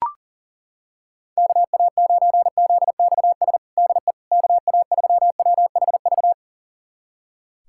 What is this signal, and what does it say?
Morse code, heard as a keyed tone, about 40 wpm, 700 Hz. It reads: KA9ZXS DE KA3WHV
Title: USA Callsign DE Callsign - 6 Characters - 40wpm